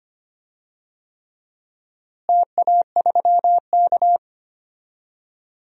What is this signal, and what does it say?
Morse code, heard as a keyed tone, about 25 wpm, 700 Hz. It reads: TA3K
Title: DXCC-100 Call Signs - 4 Characters - Repeated 3x - 25wpm